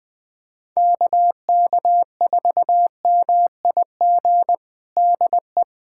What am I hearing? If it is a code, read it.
Morse code, heard as a keyed tone, about 20 wpm, 700 Hz. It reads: KK4MIG DE
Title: USA Callsign DE Callsign 20wpm